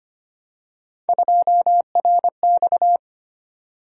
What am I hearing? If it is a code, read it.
Morse code, heard as a keyed tone, about 25 wpm, 700 Hz. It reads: 2RX
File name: Three Letter-Number Combinations - 25wpm